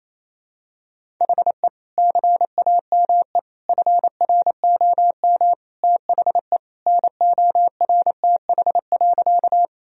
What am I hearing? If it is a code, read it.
Morse code, heard as a keyed tone, about 28 wpm, 700 Hz. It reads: HE CAME FROM THE NORTH.
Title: Sentences from Top 1000 Words - Review Part 1 - 28wpm